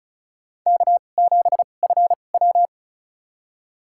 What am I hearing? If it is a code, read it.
Morse code, heard as a keyed tone, about 36 wpm, spaced as for 25 wpm, 690 Hz. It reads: K7FW
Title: USA Call Signs - 4 Characters - 2x Character Spacing - 35wpm